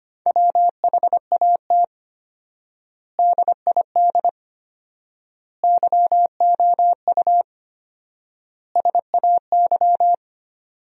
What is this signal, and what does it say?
Morse code, heard as a keyed tone, about 25 wpm, 700 Hz. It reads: WHAT DID YOU SAY
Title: 4x Word-Spacing - Sentences from Top 200 Words - 25wpm